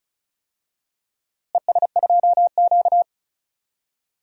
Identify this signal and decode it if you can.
Morse code, heard as a keyed tone, about 34 wpm, 700 Hz. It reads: ES2Q
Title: DXCC-100 Call Signs - 4 Characters - Repeated 2x - 35wpm